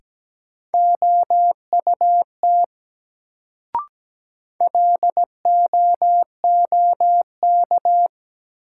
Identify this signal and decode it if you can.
Morse code, heard as a keyed tone, about 17 wpm, 700 Hz. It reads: OUT LOOK
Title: Sentences from Top 100 Words - New Word Drill - 17wpm